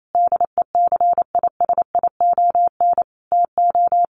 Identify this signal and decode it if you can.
Morse code, heard as a keyed tone, about 28 wpm, 700 Hz. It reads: DECISION TO
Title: Sentences from Top 1400 Words 28wpm